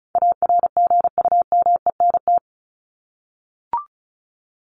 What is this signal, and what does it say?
Morse code, heard as a keyed tone, about 34 wpm, 700 Hz. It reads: ARGUMENT
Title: Top 1251-1500 Words 35wpm